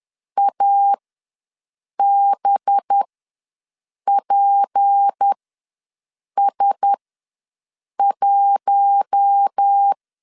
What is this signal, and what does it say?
Morse code, heard as a keyed tone, about 10 wpm, spaced as for 6 wpm, 800 Hz. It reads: ABPS1
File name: LEZIONI / 6 PKX0 / 6 PKX ws 10wpm 3706